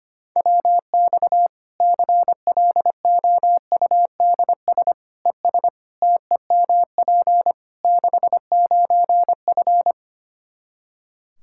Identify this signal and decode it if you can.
Morse code, heard as a keyed tone, about 25 wpm, 700 Hz. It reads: WX CLOUDS ES TEMP 69F
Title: K1USN SST QSO Practice 25wpm